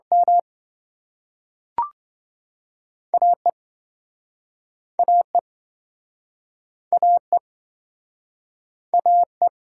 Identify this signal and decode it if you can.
Morse code, heard as a keyed tone, about 27 wpm, 700 Hz. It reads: M AE AE AE AE